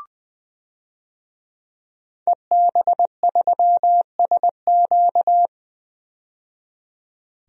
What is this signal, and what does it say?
Morse code, heard as a keyed tone, about 20 wpm, 700 Hz. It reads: EB3SQ
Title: DXCC-100 Call Signs - 5 Characters - 20wpm